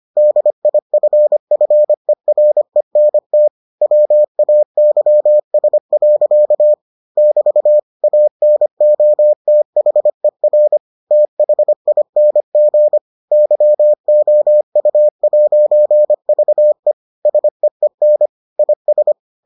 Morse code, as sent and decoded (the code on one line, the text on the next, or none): -.. .. ..-. ..-. .-. -. - / .-- .- -.-- ... .-.-.- / -...- / .- -. --- - .... .-. / - .... .. -. --. / -.-- --- ..- .----. ...- . / ... . -. / .. ...
DIFFERENT WAYS. <BT> ANOTHER THING YOU'VE SEEN IS